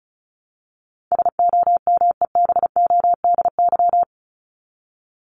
Morse code, heard as a keyed tone, about 34 wpm, 700 Hz. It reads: SOMEBODY
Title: Top 1751-2000 Words 35wpm